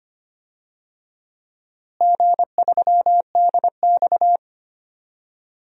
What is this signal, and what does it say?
Morse code, heard as a keyed tone, about 25 wpm, 700 Hz. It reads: G3DX
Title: DXCC-100 Call Signs - 4 Characters - Repeated 2x - 25wpm